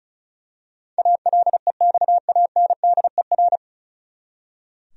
Morse code, heard as a keyed tone, about 36 wpm, 700 Hz. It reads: ALEXANDER
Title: USA Common Names - 35wpm